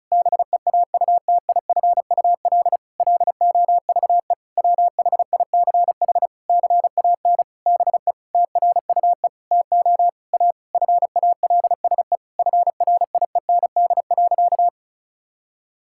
Morse code, heard as a keyed tone, about 34 wpm, 700 Hz. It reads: BEAUTIFUL LOVE WHICH CAN BE TRUE TO A FALSE FRIEND.
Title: Sentences from Top 1600 Words 35wpm